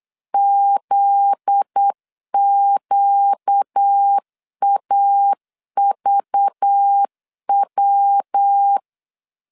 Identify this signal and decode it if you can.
Morse code, heard as a keyed tone, about 8 wpm, 800 Hz. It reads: ZQAVW
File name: LEZIONI / 7 GF9Z / 7 GF9Z 8wpm 3213